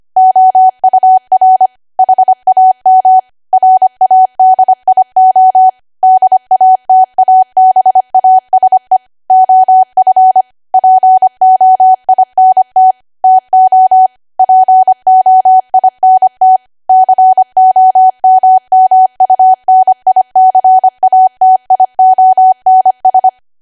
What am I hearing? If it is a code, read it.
Morse code, heard as a keyed tone, about 25 wpm, 750 Hz. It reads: OUR HAM RADIO DATABASE OF POINT TO POINT COMMUNICATIONS